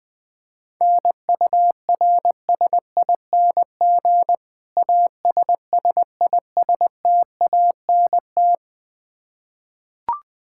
Morse code, heard as a keyed tone, about 20 wpm, 700 Hz. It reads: NURSING ASSISTANT
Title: Practice QSO - Exchange 3 - More Info - 20wpm